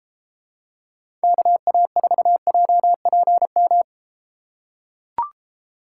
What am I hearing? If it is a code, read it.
Morse code, heard as a keyed tone, about 33 wpm, 700 Hz. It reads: KA4JPM